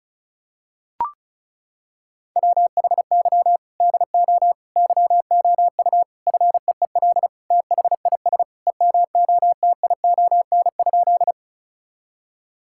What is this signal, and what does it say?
Morse code, heard as a keyed tone, about 36 wpm, 700 Hz. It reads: WHY DO YOU FEEL THIS EMOTION?
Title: Sentences from Top 1600 Words 35wpm